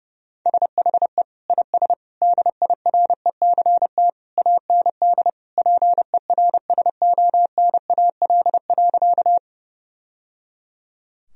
Morse code, heard as a keyed tone, about 28 wpm, 700 Hz. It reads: SHE IS DIRECT AND PERSONAL.
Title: Sentences from Top 1000 Words - Review Part 4 - 30wpm